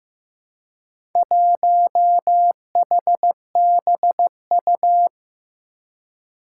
Morse code, heard as a keyed tone, about 15 wpm, 700 Hz. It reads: 1HBU
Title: Four Challenging Letter-Number Combinations 15wpm